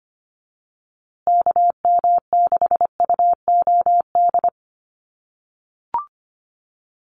Morse code, heard as a keyed tone, about 25 wpm, 700 Hz. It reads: KM6UOD
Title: USA Call Signs Repeated 2x 25wpm